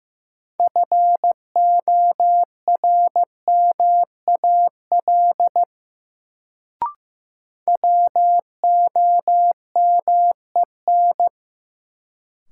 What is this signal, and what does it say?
Morse code, heard as a keyed tone, about 16 wpm, 700 Hz. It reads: FORMAL WOMEN
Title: Sentences from Top 1300 Words - New Word Drill - 15wpm